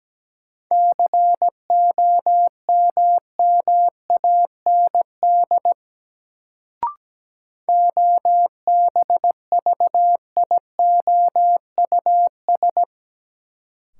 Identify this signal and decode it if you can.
Morse code, heard as a keyed tone, about 17 wpm, 700 Hz. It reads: COMMAND OBVIOUS
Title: Sentences from Top 1600 Words - New Word Drill - 17wpm